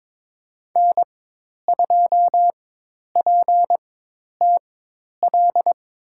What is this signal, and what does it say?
Morse code, heard as a keyed tone, about 21 wpm, spaced as for 10 wpm, 700 Hz. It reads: N2PTL
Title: USA Call Signs - 5 Characters - 4x Character Spacing - 22wpm